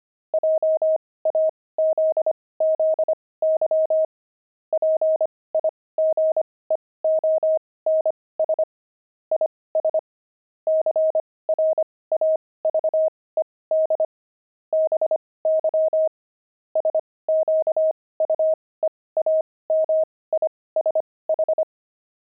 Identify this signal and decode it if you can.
Morse code, heard as a keyed tone, about 25 wpm, spaced as for 18 wpm, 630 Hz. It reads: JAZZY PIGEONS IS CRAVED BY SQUEAMISH
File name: pangram-fs-18-cs-25-pigeons